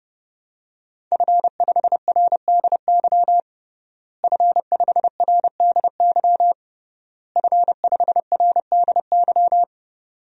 Morse code, heard as a keyed tone, about 32 wpm, 700 Hz. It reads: F5RDY F5RDY F5RDY
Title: DXCC-100 Call Signs - 5 Characters - Repeated 3x - 30wpm